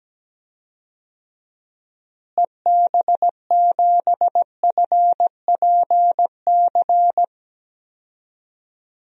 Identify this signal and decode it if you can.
Morse code, heard as a keyed tone, about 17 wpm, 700 Hz. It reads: EB7FPC
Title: DXCC-100 Call Signs - 6 Characters - Repeated 3x - 17wpm